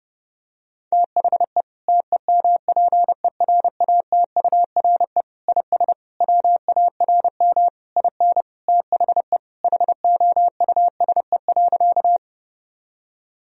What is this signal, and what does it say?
Morse code, heard as a keyed tone, about 32 wpm, 700 Hz. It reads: THE TEMPERATURE IS WARM IN THE HOUSE.
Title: Sentences from Top 1000 Words - Review Part 4 - 30wpm